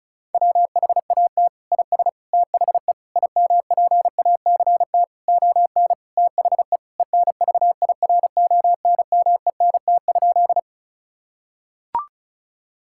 Morse code, heard as a keyed tone, about 36 wpm, 700 Hz. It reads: WHAT IS THE IMPACT ON THE ENVIRONMENT?
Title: Sentences from Top 1000 Words - Review Part 6 - 35wpm